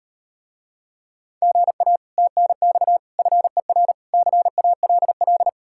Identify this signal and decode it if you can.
Morse code, heard as a keyed tone, about 38 wpm, 700 Hz. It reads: GA TNX FER CALL